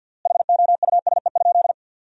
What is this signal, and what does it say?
Morse code, heard as a keyed tone, about 50 wpm, 700 Hz. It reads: HOUSE?
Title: 3x Word-Spacing - Sentences from Top 200 Words - 50wpm